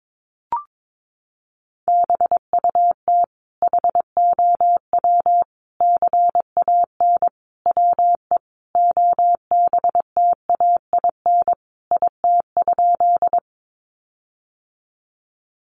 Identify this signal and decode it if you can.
Morse code, heard as a keyed tone, about 22 wpm, 700 Hz. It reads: BUT HOW CAN WE OBTAIN IT?
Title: Sentences from Top 1300 Words 22wpm